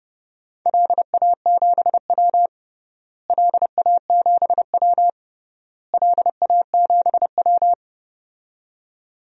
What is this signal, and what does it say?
Morse code, heard as a keyed tone, about 30 wpm, 700 Hz. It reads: LA7W LA7W LA7W
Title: DXCC-100 Call Signs - 4 Characters - Repeated 3x - 30wpm